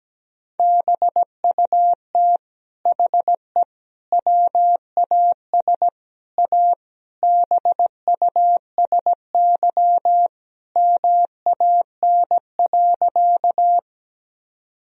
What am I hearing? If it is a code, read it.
Morse code, heard as a keyed tone, about 17 wpm, 700 Hz. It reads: BUT HE WAS A BUSY MAN.
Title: Sentences from Top 1000 Words - Review Part 3 - 17wpm